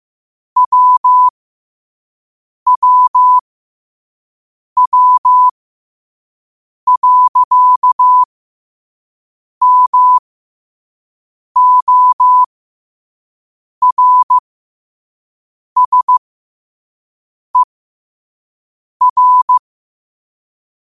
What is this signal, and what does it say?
Morse code, heard as a keyed tone, about 15 wpm, spaced as for 5 wpm, 1000 Hz. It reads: WWW.MORSER